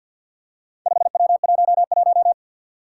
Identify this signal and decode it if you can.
Morse code, heard as a keyed tone, about 50 wpm, 700 Hz. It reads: 5W11
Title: Four Challenging Letter-Number Combinations 50wpm